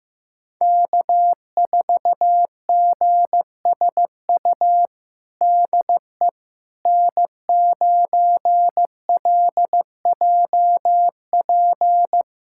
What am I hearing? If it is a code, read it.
Morse code, heard as a keyed tone, about 16 wpm, 700 Hz. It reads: K4GSU DE N9LJP
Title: USA Callsign DE Callsign 15wpm